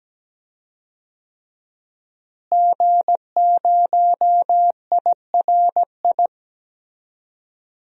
Morse code, heard as a keyed tone, about 17 wpm, 700 Hz. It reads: G0IRI
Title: DXCC-100 Call Signs - 5 Characters - Repeated 3x - 17wpm